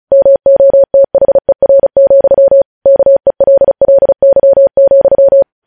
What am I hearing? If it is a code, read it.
Morse code, heard as a keyed tone, about 36 wpm, 550 Hz. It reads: MOTHER, KELLY,
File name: podcast / News headlines-1x-35x35